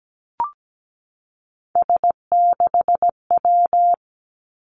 Morse code, heard as a keyed tone, about 17 wpm, 700 Hz. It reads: S6W
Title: Three Challenging Letter-Number Combinations 17wpm